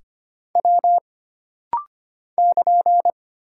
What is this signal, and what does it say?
Morse code, heard as a keyed tone, about 25 wpm, 700 Hz. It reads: W <KN>
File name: Single Character (Letter:Number:Punctuation:Prosign) - Rapid-Fire - 25wpm